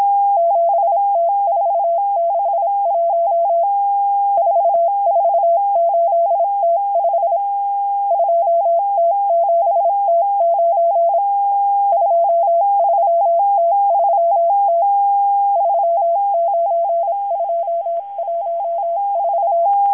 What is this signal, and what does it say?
Morse code, heard as a keyed tone, about 26 wpm, 680 Hz. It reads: 7T461 448T5 2T7T9 23T3T 39214
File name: HM02- -0437utc-7351khz